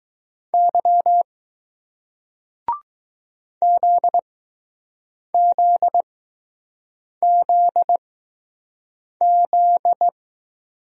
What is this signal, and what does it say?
Morse code, heard as a keyed tone, about 21 wpm, spaced as for 6 wpm, 700 Hz. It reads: Y ZZZZ